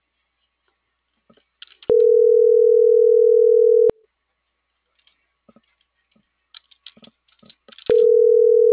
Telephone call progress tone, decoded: ringback tone

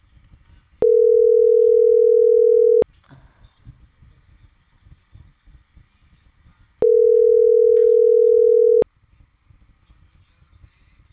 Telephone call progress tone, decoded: ringback tone

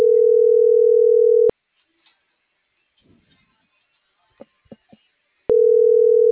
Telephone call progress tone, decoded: ringback tone

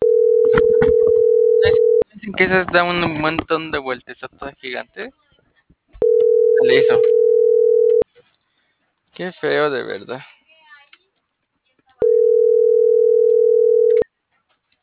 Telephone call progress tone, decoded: ringback tone

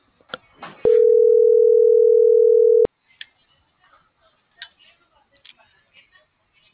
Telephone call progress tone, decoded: ringback tone

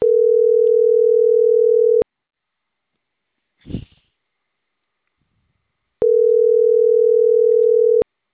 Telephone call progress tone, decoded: ringback tone

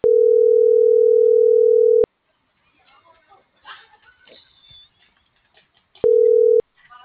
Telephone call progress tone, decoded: ringback tone

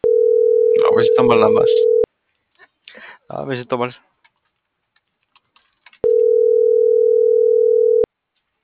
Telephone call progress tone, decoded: ringback tone

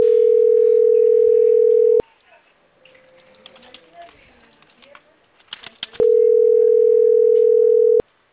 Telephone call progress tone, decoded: ringback tone

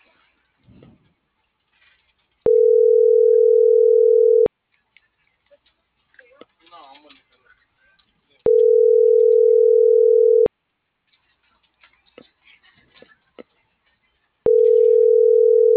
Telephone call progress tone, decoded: ringback tone

